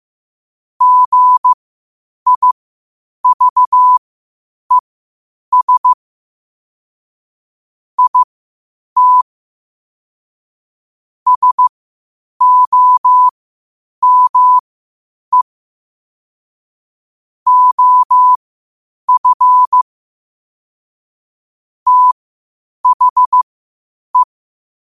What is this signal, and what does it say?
Morse code, heard as a keyed tone, about 15 wpm, spaced as for 8 wpm, 1000 Hz. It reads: GIVES IT SOME OF THE
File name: qotdfiles / 8wpm